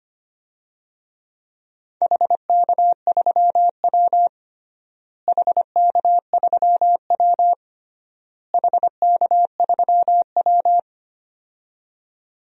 Morse code, heard as a keyed tone, about 25 wpm, 700 Hz. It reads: HK3W HK3W HK3W
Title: DXCC-100 Call Signs - 4 Characters - Repeated 3x - 25wpm